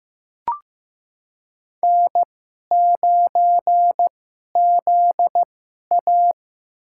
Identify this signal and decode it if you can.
Morse code, heard as a keyed tone, about 15 wpm, spaced as for 11 wpm, 700 Hz. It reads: N9ZA
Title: USA Call Signs - 4 Characters - 2x Character Spacing - 15wpm